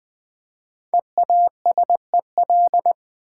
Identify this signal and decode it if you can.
Morse code, heard as a keyed tone, about 20 wpm, 700 Hz. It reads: EASEL
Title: Easy Words with Hints - 20wpm